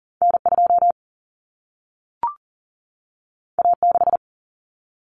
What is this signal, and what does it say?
Morse code, heard as a keyed tone, about 40 wpm, 700 Hz. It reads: N2 A6
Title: Common Cut Numbers - 2-Digit 40wpm